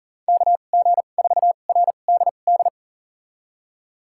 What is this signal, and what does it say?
Morse code, heard as a keyed tone, about 40 wpm, spaced as for 29 wpm, 700 Hz. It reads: KG4RDD
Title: USA Call Signs - 6 Characters - 2x Character Spacing - 40wpm